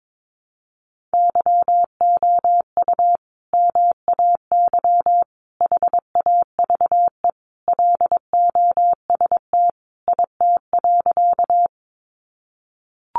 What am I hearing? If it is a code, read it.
Morse code, heard as a keyed tone, about 22 wpm, 700 Hz. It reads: YOU MAY HAVE LOST IT.
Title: Sentences from Top 1400 Words 22wpm